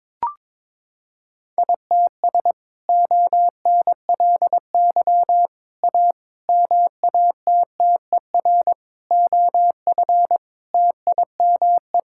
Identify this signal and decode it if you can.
Morse code, heard as a keyed tone, about 22 wpm, 700 Hz. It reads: ITS ONLY A MATTER OF TIME
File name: practice / 22wpm / Sets of 7 Words - 22wpm